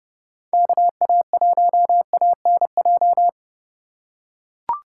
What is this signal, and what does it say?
Morse code, heard as a keyed tone, about 31 wpm, 700 Hz. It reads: KA1ANJ